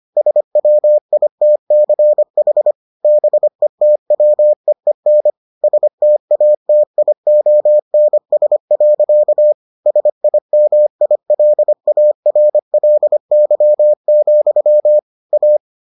Morse code, heard as a keyed tone, about 25 wpm, 600 Hz. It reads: SWITCH BETWEEN STATIONS. SIMILARLY, A